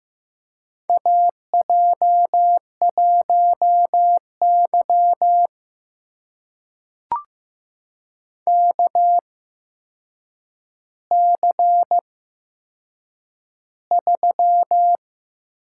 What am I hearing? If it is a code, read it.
Morse code, heard as a keyed tone, about 15 wpm, spaced as for 7 wpm, 700 Hz. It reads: AJ1Y KC3
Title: USA Call Signs - 8x Character Spacing - 15wpm